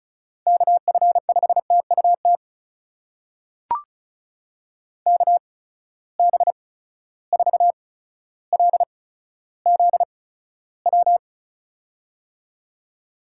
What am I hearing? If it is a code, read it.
Morse code, heard as a keyed tone, about 36 wpm, spaced as for 15 wpm, 700 Hz. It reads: KF5TUT KB4LZW
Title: USA Call Signs - 6 Characters - 8x Character Spacing - 35wpm